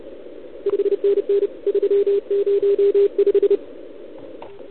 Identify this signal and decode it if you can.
Morse code, heard as a keyed tone, about 30 wpm, 400 Hz. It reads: 5NN 305
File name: ARCHIV / OKDX2019 / OKOM40 KX3